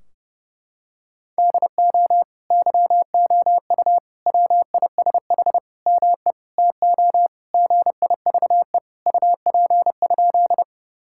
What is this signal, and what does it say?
Morse code, heard as a keyed tone, about 30 wpm, 700 Hz. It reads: DO YOU WISH ME TO GIVE UP?
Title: Sentences from Top 1600 Words 30wpm